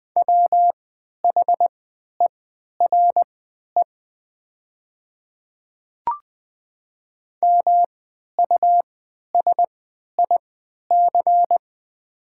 Morse code, heard as a keyed tone, about 20 wpm, spaced as for 11 wpm, 700 Hz. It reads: WHERE MUSIC
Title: 3x Character Spacing - Top 500 Words - 20wpm